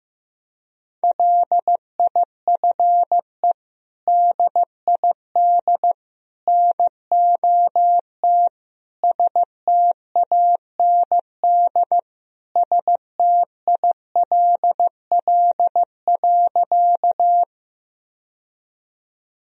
Morse code, heard as a keyed tone, about 15 wpm, 700 Hz. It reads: LIFE DID NOT STAND STILL.
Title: Sentences from Top 1000 Words - Review Part 1 - 15wpm